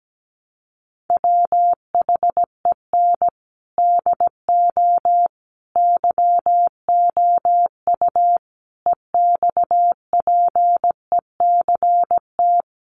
Morse code, heard as a keyed tone, about 17 wpm, 700 Hz. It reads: WHEN DO YOU EXPECT
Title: Sentences from Top 1100 Words 17wpm